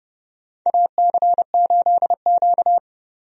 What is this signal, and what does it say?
Morse code, heard as a keyed tone, about 30 wpm, 700 Hz. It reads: AC8Q
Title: USA Call Signs - 4 Characters - 30wpm